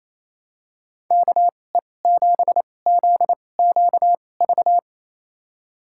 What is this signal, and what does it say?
Morse code, heard as a keyed tone, about 28 wpm, spaced as for 20 wpm, 700 Hz. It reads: KE7ZQV
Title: USA Call Signs - 6 Characters - 2x Character Spacing - 28wpm